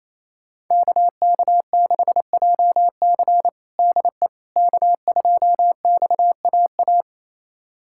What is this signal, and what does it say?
Morse code, heard as a keyed tone, about 28 wpm, 700 Hz. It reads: KK6JC DE K2XAA
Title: USA Callsign DE Callsign - 5 Characters - 28wpm